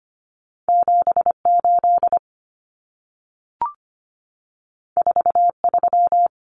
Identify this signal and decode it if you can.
Morse code, heard as a keyed tone, about 25 wpm, 700 Hz. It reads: 78 43
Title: Common Cut Numbers - 2-Digit 25wpm